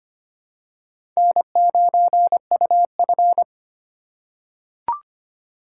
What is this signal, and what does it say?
Morse code, heard as a keyed tone, about 25 wpm, 700 Hz. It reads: N9UF
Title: USA Call Signs - 4 Characters - 2x Character Spacing - 25wpm